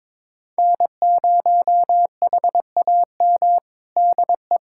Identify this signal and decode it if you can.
Morse code, heard as a keyed tone, about 21 wpm, 700 Hz. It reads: N0HAM DE
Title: USA Callsign DE Callsign - 5 Characters - 22wpm